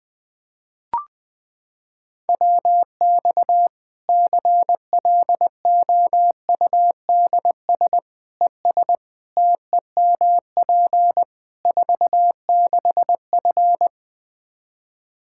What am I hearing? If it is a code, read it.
Morse code, heard as a keyed tone, about 20 wpm, 700 Hz. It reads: WX CLOUDS ES TEMP 46F
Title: Practice QSO - Exchange 2 - Info - 20wpm